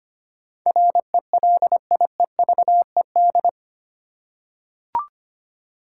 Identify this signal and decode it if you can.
Morse code, heard as a keyed tone, about 25 wpm, 700 Hz. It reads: RELIEVED
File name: Eight Letter Words - 25wpm